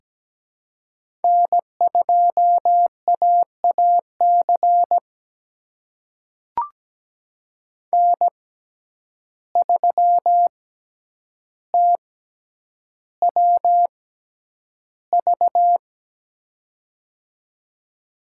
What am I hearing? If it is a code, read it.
Morse code, heard as a keyed tone, about 17 wpm, spaced as for 9 wpm, 700 Hz. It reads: N2AAC N3TWV
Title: USA Call Signs - 5 Characters - 6x Character Spacing - 17wpm